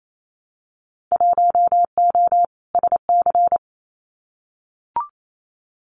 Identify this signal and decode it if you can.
Morse code, heard as a keyed tone, about 28 wpm, 700 Hz. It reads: 1O SC
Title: Winter Field Day Exchanges 28wpm